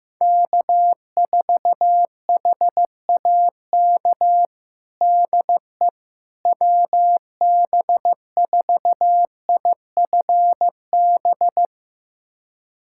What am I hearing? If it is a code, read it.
Morse code, heard as a keyed tone, about 16 wpm, 700 Hz. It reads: K4HAK DE WB4IFB
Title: USA Callsign DE Callsign 15wpm